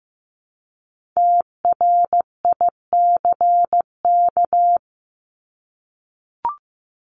Morse code, heard as a keyed tone, about 15 wpm, 700 Hz. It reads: TRICK
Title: Top 1251-1500 Words 15wpm